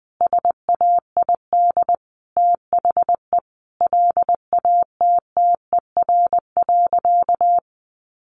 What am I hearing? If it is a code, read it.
Morse code, heard as a keyed tone, about 20 wpm, 700 Hz. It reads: SAID THE LATTER.
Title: A Sound Of Thunder 20wpm